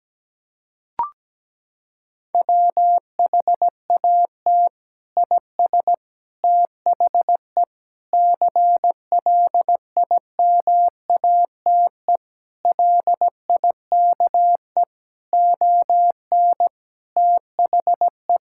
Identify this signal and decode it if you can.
Morse code, heard as a keyed tone, about 17 wpm, 700 Hz. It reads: WHAT IS THE CLIMATE LIKE ON THE
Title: Sentences from Top 1600 Words 17wpm